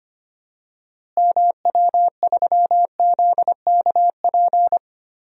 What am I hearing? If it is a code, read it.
Morse code, heard as a keyed tone, about 25 wpm, 700 Hz. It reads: MW3ZKP
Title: DXCC-100 Call Signs - 6 Characters - Repeated 2x - 25wpm